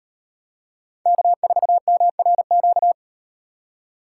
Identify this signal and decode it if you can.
Morse code, heard as a keyed tone, about 36 wpm, 700 Hz. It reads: K4MRQ